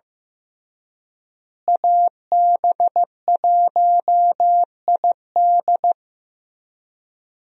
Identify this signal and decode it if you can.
Morse code, heard as a keyed tone, about 15 wpm, 700 Hz. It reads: AB1ID
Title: USA Call Signs - 5 Characters - Repeated 3x - 15wpm